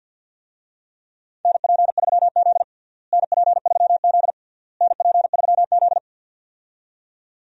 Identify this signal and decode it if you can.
Morse code, heard as a keyed tone, about 52 wpm, 700 Hz. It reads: NP3Z NP3Z NP3Z
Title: USA Call Signs - 4 Characters - Repeated 3x - 50wpm